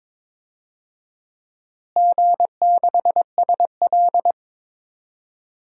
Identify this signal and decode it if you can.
Morse code, heard as a keyed tone, about 22 wpm, 700 Hz. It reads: G6SL
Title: DXCC-100 Call Signs - 4 Characters - Repeated 2x - 22wpm